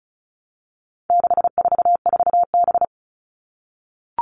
Morse code, heard as a keyed tone, about 34 wpm, 700 Hz. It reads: 644B
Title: All Cut Numbers - 4-Digit 35wpm